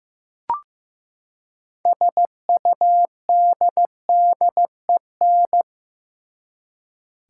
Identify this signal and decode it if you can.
Morse code, heard as a keyed tone, about 15 wpm, 700 Hz. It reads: SUDDEN
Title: Six Letter Words - 15wpm